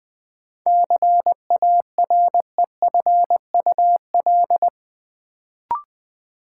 Morse code, heard as a keyed tone, about 20 wpm, 700 Hz. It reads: CAREFUL
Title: Sentences from Top 1600 Words - New Word Drill - 20wpm